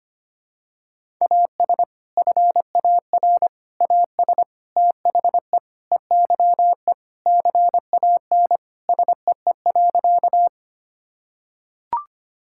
Morse code, heard as a keyed tone, about 25 wpm, 700 Hz. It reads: AS FAR AS THE EYE CAN SEE.
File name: practice / 25wpm / Sentences with 4-Word Phrases - 25wpm